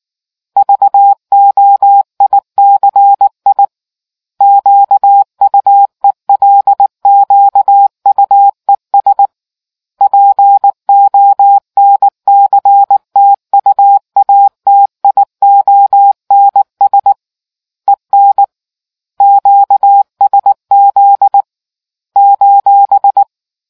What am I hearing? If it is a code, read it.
Morse code, heard as a keyed tone, about 19 wpm, 800 Hz. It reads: VOICI QUELQUES PONCTUATIONS EN QSZ :